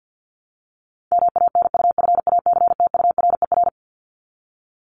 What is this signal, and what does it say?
Morse code, heard as a keyed tone, about 50 wpm, 700 Hz. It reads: MANUFACTURER
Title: Top 1751-2000 Words 50wpm